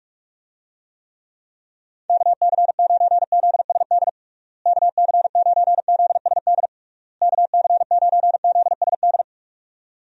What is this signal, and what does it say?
Morse code, heard as a keyed tone, about 46 wpm, 700 Hz. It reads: KC9ZSD KC9ZSD KC9ZSD
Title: USA Call Signs - 6 Characters - Repeated 3x - 45wpm